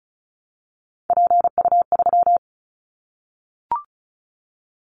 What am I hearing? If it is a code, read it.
Morse code, heard as a keyed tone, about 36 wpm, 700 Hz. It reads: PU3
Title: Callsign Prefixes for Top 50 DXCC - 35wpm